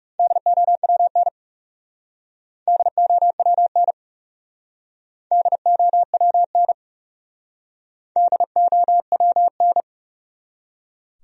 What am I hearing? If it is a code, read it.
Morse code, heard as a keyed tone, about 36 wpm, 700 Hz. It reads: DOWN DOWN DOWN DOWN
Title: Speed-Racing Top 500 Words 30wpm